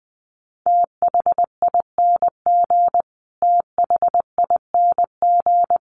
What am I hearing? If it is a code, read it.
Morse code, heard as a keyed tone, about 20 wpm, 700 Hz. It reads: THING THING
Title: Top 500 Words Repeated 20wpm